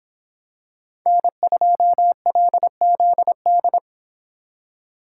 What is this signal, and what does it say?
Morse code, heard as a keyed tone, about 23 wpm, 700 Hz. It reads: N2LZD